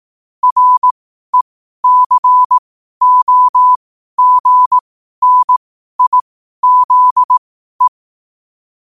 Morse code, heard as a keyed tone, about 18 wpm, spaced as for 12 wpm, 1000 Hz. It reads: RECOGNIZE
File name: qotdfiles / 12wpm